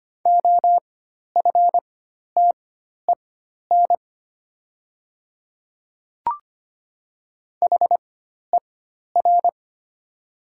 Morse code, heard as a keyed tone, about 25 wpm, spaced as for 12 wpm, 700 Hz. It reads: OFTEN HER
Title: 4x Character Spacing - Top 300 Words - 25wpm